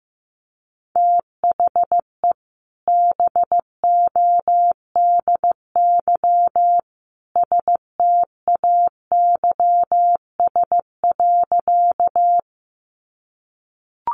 Morse code, heard as a keyed tone, about 15 wpm, 700 Hz. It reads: THE BODY STAYS.
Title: A Sound Of Thunder 15wpm